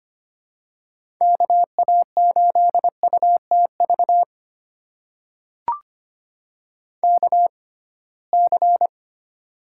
Morse code, heard as a keyed tone, about 25 wpm, spaced as for 19 wpm, 700 Hz. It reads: KA8UTV KC
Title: USA Call Signs - 6 Characters - 6x Character Spacing - 25wpm